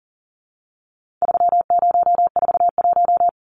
Morse code, heard as a keyed tone, about 40 wpm, 700 Hz. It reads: 3041
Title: Numbers - 4-Digit 40wpm